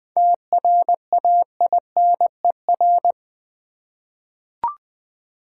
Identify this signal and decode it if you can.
Morse code, heard as a keyed tone, about 20 wpm, 690 Hz. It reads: TRAINER
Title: 7 Letter Words - First 3 Letters Given - 20wpm